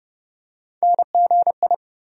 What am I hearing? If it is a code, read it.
Morse code, heard as a keyed tone, about 30 wpm, 700 Hz. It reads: NGI
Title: USA Call Sign Postfixes - 3 Letters 30wpm